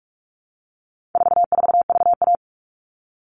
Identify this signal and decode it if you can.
Morse code, heard as a keyed tone, about 44 wpm, 700 Hz. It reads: 44VA
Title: All Cut Numbers - 4-Digit 45wpm